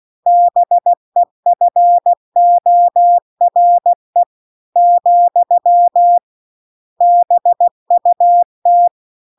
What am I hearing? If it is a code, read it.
Morse code, heard as a keyed tone, about 16 wpm, 700 Hz. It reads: BEFORE , BUT